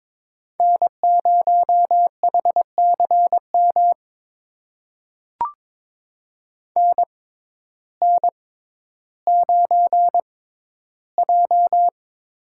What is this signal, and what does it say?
Morse code, heard as a keyed tone, about 22 wpm, spaced as for 12 wpm, 700 Hz. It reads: N0HCM NN9J
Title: USA Call Signs - 6x Character Spacing - 22wpm